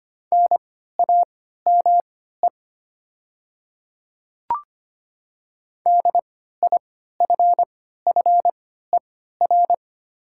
Morse code, heard as a keyed tone, about 25 wpm, spaced as for 14 wpm, 700 Hz. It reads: NAME DIFFER
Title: 3x Character Spacing - Top 300 Words - 25wpm